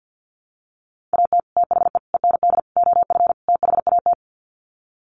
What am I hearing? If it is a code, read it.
Morse code, heard as a keyed tone, about 50 wpm, 700 Hz. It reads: AT THE END OF THAT
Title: Sets of 5 Words Encore2 50wpm